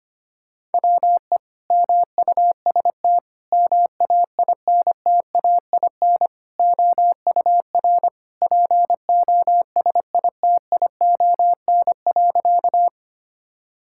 Morse code, heard as a keyed tone, about 25 wpm, 700 Hz. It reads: WE MUST MAINTAIN OUR POSITION.
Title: Sentences from Top 1000 Words - Review Part 5 - 25wpm